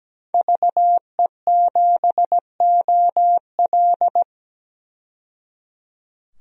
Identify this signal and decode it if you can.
Morse code, heard as a keyed tone, about 17 wpm, 700 Hz. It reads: VE7OL
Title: DXCC-100 Call Signs - 5 Characters - 17wpm